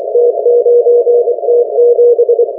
Morse code, heard as a keyed tone, about 24 wpm, 480 Hz. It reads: T9T7